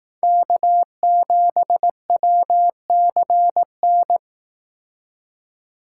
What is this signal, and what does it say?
Morse code, heard as a keyed tone, about 18 wpm, 700 Hz. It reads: K7WCN
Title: Speed Racing - USA 1x3 Call Signs 15wpm